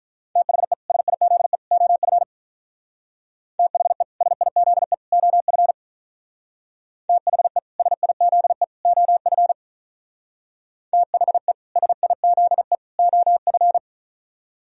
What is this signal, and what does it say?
Morse code, heard as a keyed tone, about 44 wpm, 700 Hz. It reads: THE SIZE OF THE SIZE OF THE SIZE OF THE SIZE OF